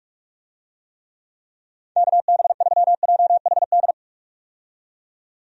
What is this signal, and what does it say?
Morse code, heard as a keyed tone, about 46 wpm, 700 Hz. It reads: KB3JHD
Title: USA Call Signs - 6 Characters - 45wpm